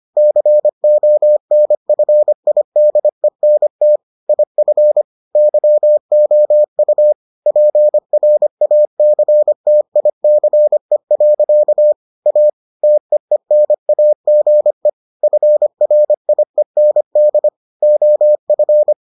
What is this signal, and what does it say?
Morse code, heard as a keyed tone, about 25 wpm, 600 Hz. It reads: CONFIDENT IF YOU PRACTICE. A TEENAGE FRIEND OF